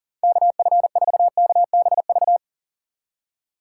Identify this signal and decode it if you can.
Morse code, heard as a keyed tone, about 40 wpm, 700 Hz. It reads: KF4KBV
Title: USA Call Signs - 6 Characters - 3x Character Spacing - 40wpm